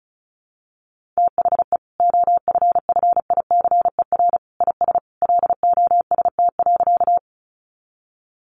Morse code, heard as a keyed tone, about 34 wpm, 700 Hz. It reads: THE OFFICER IS LOST.
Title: Sentences from Top 1400 Words 35wpm